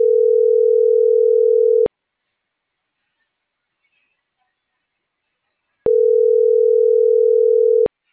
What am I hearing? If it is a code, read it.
Ringback tone (North America), 440 + 480 Hz.